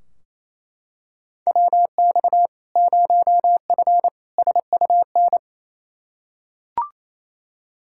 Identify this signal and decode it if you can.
Morse code, heard as a keyed tone, about 28 wpm, 700 Hz. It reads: WX 0F SUN
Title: Practice QSO - Exchange 2 - Info - 28wpm